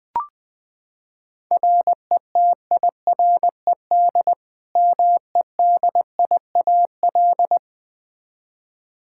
Morse code, heard as a keyed tone, about 20 wpm, 700 Hz. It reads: RETIRED MEDIAL
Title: Practice QSO - Exchange 3 - More Info - 20wpm